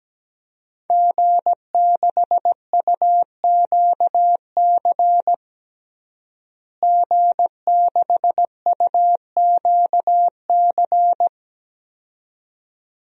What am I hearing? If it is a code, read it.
Morse code, heard as a keyed tone, about 17 wpm, 700 Hz. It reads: G6UQC G6UQC
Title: DXCC-100 Call Signs - 5 Characters - Repeated 2x - 17wpm